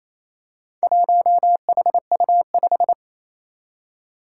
Morse code, heard as a keyed tone, about 28 wpm, 700 Hz. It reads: 1HU5
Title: Four Challenging Letter-Number Combinations 28wpm